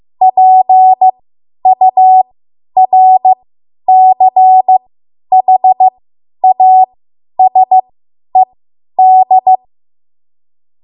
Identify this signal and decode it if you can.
Morse code, heard as a keyed tone, about 15 wpm, spaced as for 10 wpm, 750 Hz. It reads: PURCHASED
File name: Morse / 10WPM